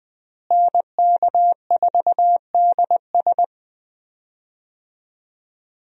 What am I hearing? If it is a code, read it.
Morse code, heard as a keyed tone, about 20 wpm, 700 Hz. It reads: NK4DS
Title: USA Call Signs - 5 Characters - Repeated 2x - 20wpm